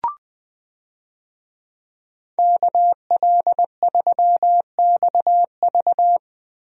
Morse code, heard as a keyed tone, about 20 wpm, 700 Hz. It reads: KL3XV
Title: USA Call Signs - 5 Characters - Repeated 3x - 20wpm